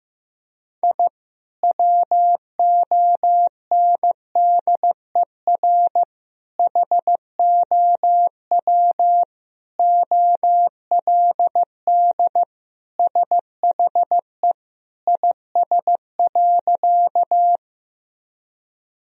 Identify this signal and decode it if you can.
Morse code, heard as a keyed tone, about 15 wpm, 700 Hz. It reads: I WONDER HOW OLD SHE IS.
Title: Sentences from Top 1000 Words - Review Part 3 - 15wpm